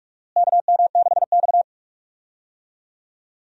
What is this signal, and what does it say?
Morse code, heard as a keyed tone, about 47 wpm, 700 Hz. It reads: KM6X
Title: USA Call Signs - 4 Characters - Repeated 3x - 45wpm